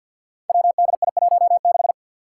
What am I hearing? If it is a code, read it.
Morse code, heard as a keyed tone, about 47 wpm, 700 Hz. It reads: WDI16
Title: Five Challenging Letter-Number Combinations 50wpm